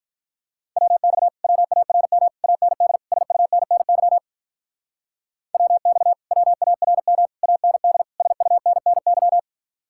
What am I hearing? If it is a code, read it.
Morse code, heard as a keyed tone, about 44 wpm, 700 Hz. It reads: WX WARM AND SUNNY WX WARM AND SUNNY